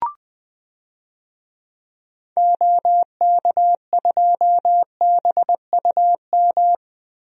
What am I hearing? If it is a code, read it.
Morse code, heard as a keyed tone, about 20 wpm, 700 Hz. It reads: OK2BUM
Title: DXCC-100 Call Signs - 6 Characters - 20wpm